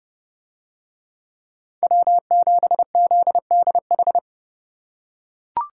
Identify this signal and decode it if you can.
Morse code, heard as a keyed tone, about 29 wpm, 700 Hz. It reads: W7ZDH